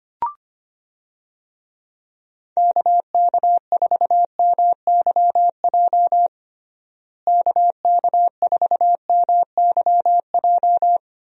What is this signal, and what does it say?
Morse code, heard as a keyed tone, about 25 wpm, 700 Hz. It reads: KK4MYJ KK4MYJ
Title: USA Call Signs - 6 Characters - Repeated 2x - 25wpm